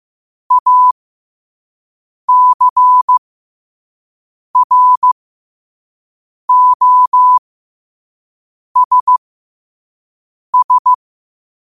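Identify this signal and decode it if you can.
Morse code, heard as a keyed tone, about 15 wpm, spaced as for 5 wpm, 1000 Hz. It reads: ACROSS
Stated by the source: Morse code practice at 5 wpm.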